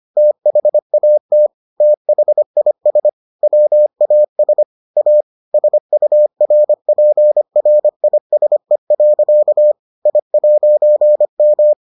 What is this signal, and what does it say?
Morse code, heard as a keyed tone, about 25 wpm, 600 Hz. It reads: THAT THIS WAS A SURPRISE. I'M